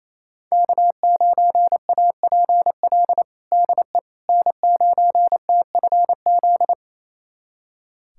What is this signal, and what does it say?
Morse code, heard as a keyed tone, about 27 wpm, 700 Hz. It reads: K9APL DE N9TFZ
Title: USA Callsign DE Callsign - 5 Characters - 28wpm